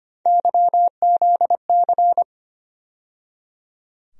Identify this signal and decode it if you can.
Morse code, heard as a keyed tone, about 25 wpm, 700 Hz. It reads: YZC
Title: USA Call Sign Postfixes - 3 Letters 25wpm